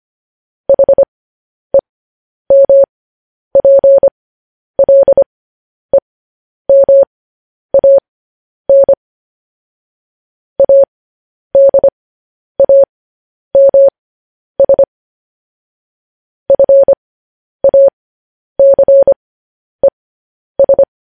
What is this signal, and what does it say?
Morse code, heard as a keyed tone, about 25 wpm, spaced as for 10 wpm, 550 Hz. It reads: HEMPLEMAN ADAMS FACES
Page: news headlines in morse code at 10 wpm